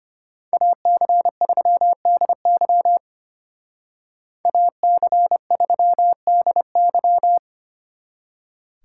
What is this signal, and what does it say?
Morse code, heard as a keyed tone, about 29 wpm, 700 Hz. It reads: AC3DY AC3DY
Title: Speed Racing - USA 2x2 Call Signs 25wpm